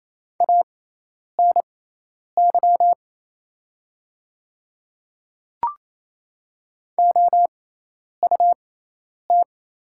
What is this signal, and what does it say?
Morse code, heard as a keyed tone, about 28 wpm, spaced as for 10 wpm, 700 Hz. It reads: ANY OUT
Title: 6x Character Spacing - Top 200 Words - 28wpm